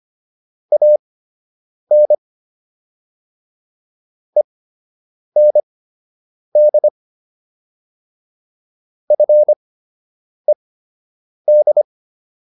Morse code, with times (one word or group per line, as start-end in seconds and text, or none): 0.7-2.1 AN
4.4-6.9 END
9.1-11.8 FED